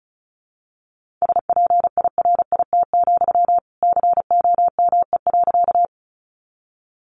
Morse code, heard as a keyed tone, about 34 wpm, 700 Hz. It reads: SPIRIT, COME.
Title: Sentences from Top 1400 Words 35wpm